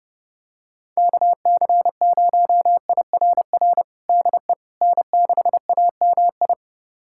Morse code, heard as a keyed tone, about 30 wpm, 700 Hz. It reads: KC0IRR DE N6AMI
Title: USA Callsign DE Callsign 30wpm